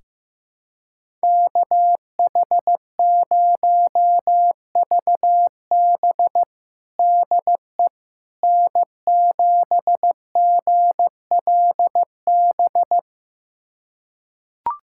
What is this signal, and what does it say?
Morse code, heard as a keyed tone, about 16 wpm, 700 Hz. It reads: KH0VB DE N7GLB
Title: USA Callsign DE Callsign - 5 Characters - 15wpm